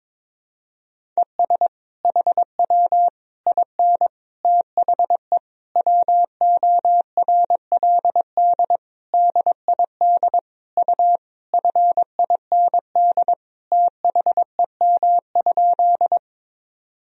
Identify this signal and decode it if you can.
Morse code, heard as a keyed tone, about 22 wpm, 700 Hz. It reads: ES HW IN THE WORLD DID U FIND THEM?
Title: Sentences from Top 200 Words and Abbreviations 22wpm